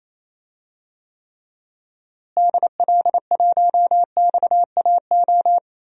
Morse code, heard as a keyed tone, about 28 wpm, 700 Hz. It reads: DL1XAO
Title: DXCC-100 Call Signs - 6 Characters - Repeated 2x - 28wpm